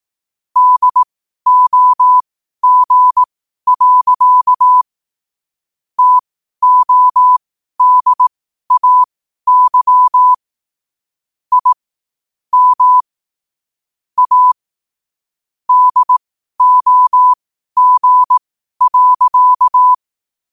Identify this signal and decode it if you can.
Morse code, heard as a keyed tone, about 18 wpm, spaced as for 12 wpm, 1000 Hz. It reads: DOG. TODAY I M A DOG.
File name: qotdfiles / 12wpm